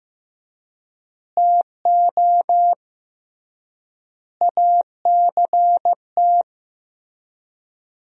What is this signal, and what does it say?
Morse code, heard as a keyed tone, about 15 wpm, 700 Hz. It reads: TO ACT
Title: 3x Word-Spacing - Sentences from Top 500 Words - 15wpm